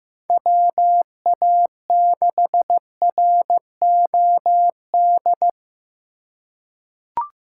Morse code, heard as a keyed tone, about 15 wpm, 700 Hz. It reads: WA6ROD
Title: USA Call Signs - 6 Characters - 4x Character Spacing - 15wpm